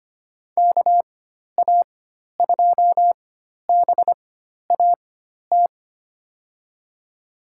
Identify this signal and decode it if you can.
Morse code, heard as a keyed tone, about 25 wpm, spaced as for 12 wpm, 700 Hz. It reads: KA2BAT
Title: USA Call Signs - 6 Characters - 4x Character Spacing - 25wpm